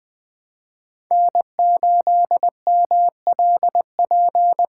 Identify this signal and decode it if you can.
Morse code, heard as a keyed tone, about 20 wpm, 700 Hz. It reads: N8MLP
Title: USA Call Signs - 5 Characters - 3x Character Spacing - 20wpm